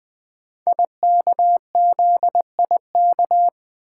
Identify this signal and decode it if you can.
Morse code, heard as a keyed tone, about 20 wpm, 700 Hz. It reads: IKZIK
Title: Code Groups - 5 Letters - 20wpm